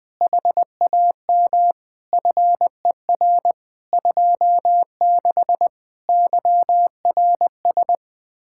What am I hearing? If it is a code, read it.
Morse code, heard as a keyed tone, about 20 wpm, 700 Hz. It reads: HAM FER 26 YRS
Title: Speed-Racing Common QSO Phrases - 17wpm